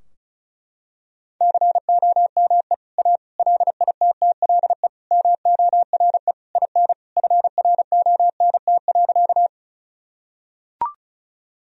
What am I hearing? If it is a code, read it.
Morse code, heard as a keyed tone, about 36 wpm, 700 Hz. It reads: COME A LITTLE MORE IN FRONT.
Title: Sentences from Top 1000 Words - Review Part 2 - 35wpm